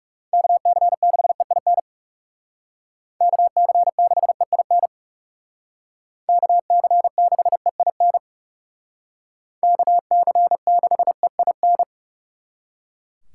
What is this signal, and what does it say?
Morse code, heard as a keyed tone, about 39 wpm, 700 Hz. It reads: KC6EIN KC6EIN KC6EIN KC6EIN